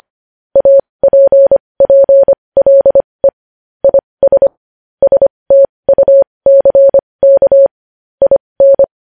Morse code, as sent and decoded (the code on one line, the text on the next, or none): .- .--. .--. .-.. / .. ... / ... - ..- -.-. -.- / .. -.
APPLE IS STUCK IN